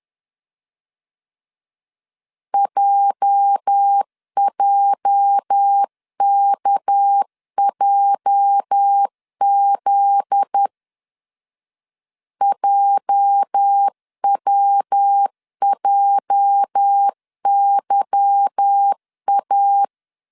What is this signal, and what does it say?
Morse code, heard as a keyed tone, about 10 wpm, 800 Hz. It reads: JJKJZ JWJYA
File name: LEZIONI / 8 YJ45 / 8 YJ 10wpm 2651